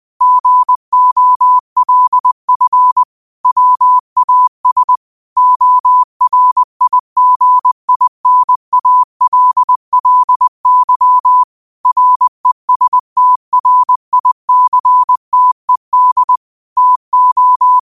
Morse code, heard as a keyed tone, about 20 wpm, 1000 Hz. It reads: GOLF WAS ORIGINALLY RESTRICTED TO
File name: qotdfiles / 20wpm